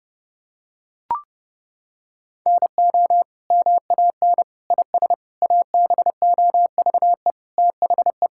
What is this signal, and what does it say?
Morse code, heard as a keyed tone, about 30 wpm, 700 Hz. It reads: NO MAN IS ABOVE THE
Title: Sentences from Top 1000 Words - Review Part 5 - 30wpm